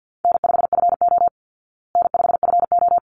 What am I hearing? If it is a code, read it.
Morse code, heard as a keyed tone, about 50 wpm, 700 Hz. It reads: N5FO N5FO
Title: Easy Call Signs Repeated 2x - 50wpm